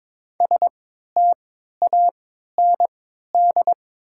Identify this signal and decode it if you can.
Morse code, heard as a keyed tone, about 22 wpm, spaced as for 13 wpm, 700 Hz. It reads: STAND
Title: 3x Character Spacing - Top 300 Words - 22wpm